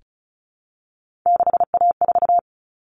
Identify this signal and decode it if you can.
Morse code, heard as a keyed tone, about 34 wpm, 700 Hz. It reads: 6A4
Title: Common Cut Numbers - 3-Digit 35wpm